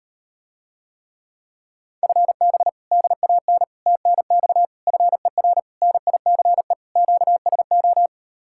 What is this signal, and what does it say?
Morse code, heard as a keyed tone, about 38 wpm, 700 Hz. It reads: FB DAN TNX FER NICE QSO